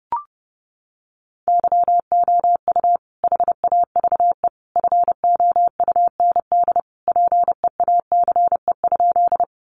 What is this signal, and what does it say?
Morse code, heard as a keyed tone, about 28 wpm, 700 Hz. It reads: YOU HAVE FOUND PEACE?
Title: Sentences from Top 1500 Words 30wpm